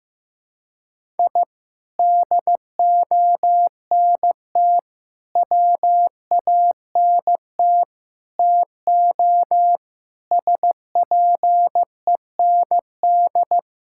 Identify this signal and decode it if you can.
Morse code, heard as a keyed tone, about 15 wpm, 700 Hz. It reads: I DONT WANT TO SPEND
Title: Sentences from Top 1000 Words - Review Part 5 - 15wpm